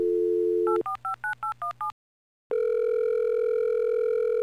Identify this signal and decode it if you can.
DTMF, heard as telephone touch tones, tones of about 120 ms. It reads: 8459817